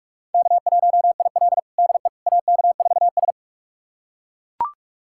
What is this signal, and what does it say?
Morse code, heard as a keyed tone, about 41 wpm, 700 Hz. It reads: K1IL DE AK4S
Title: USA Callsign DE Callsign - 4 Characters - 45wpm